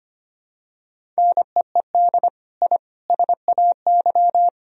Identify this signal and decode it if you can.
Morse code, heard as a keyed tone, about 25 wpm, 700 Hz. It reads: NEED I SAY
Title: Sentences from Top 1000 Words - Review Part 1 - 25wpm